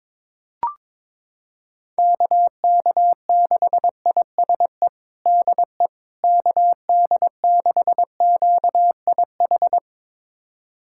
Morse code, heard as a keyed tone, about 22 wpm, 700 Hz. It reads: KK6ISE DE KD6QIH
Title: USA Callsign DE Callsign - 6 Characters - 22wpm